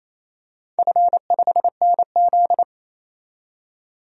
Morse code, heard as a keyed tone, about 28 wpm, 700 Hz. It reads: F5NZ
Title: DXCC-100 Call Signs - 4 Characters - 28wpm